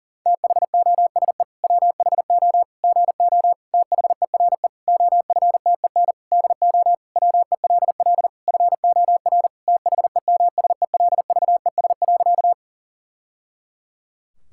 Morse code, heard as a keyed tone, about 40 wpm, 700 Hz. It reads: THOSE WHO GO THERE OFTEN DO WELL FOR THEMSELVES.
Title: Sentences from Top 1000 Words - Review Part 5 - 40wpm